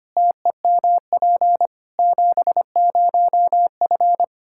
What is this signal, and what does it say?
Morse code, heard as a keyed tone, about 25 wpm, 700 Hz. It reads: TEMP 70F
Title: Practice QSO - Exchange 2 - Info - 25wpm